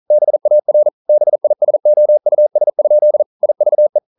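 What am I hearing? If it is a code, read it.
Morse code, heard as a keyed tone, about 42 wpm, 600 Hz. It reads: BAR BISOUS? IVE
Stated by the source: Morse code practice at 41 wpm, from CW med Gnister.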